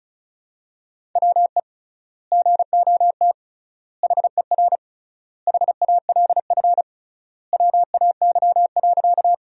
Morse code, heard as a keyed tone, about 36 wpm, 700 Hz. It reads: WE GOT HER HALF WAY.
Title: 3x Word-Spacing - Sentences from Top 400 Words - 35wpm